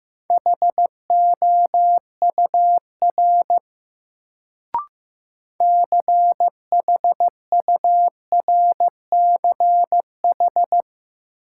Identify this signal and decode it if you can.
Morse code, heard as a keyed tone, about 15 wpm, 700 Hz. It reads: HOUR CHURCH
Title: Sets of 5 Words - New Word Drill - 15wpm